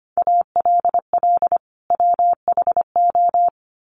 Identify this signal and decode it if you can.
Morse code, heard as a keyed tone, about 25 wpm, 700 Hz. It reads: ALL WHO
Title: Sentences from Top 1400 Words 25wpm